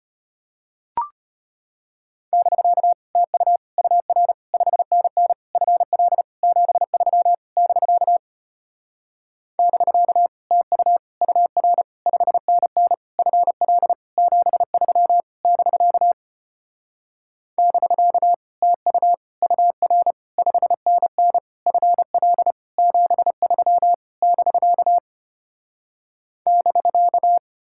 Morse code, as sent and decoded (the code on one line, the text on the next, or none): -...-.- / - ..- / ..- .-. / ..... -. -. / ..-. .-.. / --... ...-- / -...-.- / -...-.- / - ..- / ..- .-. / ..... -. -. / ..-. .-.. / --... ...-- / -...-.- / -...-.- / - ..- / ..- .-. / ..... -. -. / ..-. .-.. / --... ...-- / -...-.- / -...-.-
<BK> TU UR 5NN FL 73 <BK> <BK> TU UR 5NN FL 73 <BK> <BK> TU UR 5NN FL 73 <BK> <BK>